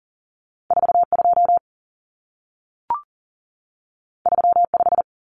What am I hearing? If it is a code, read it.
Morse code, heard as a keyed tone, about 40 wpm, 700 Hz. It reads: 42 35
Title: Numbers - 2-Digit 40wpm